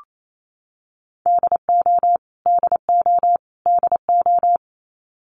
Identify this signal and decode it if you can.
Morse code, heard as a keyed tone, about 28 wpm, 700 Hz. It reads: DO DO DO
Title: Top 200 Words Repeated 28wpm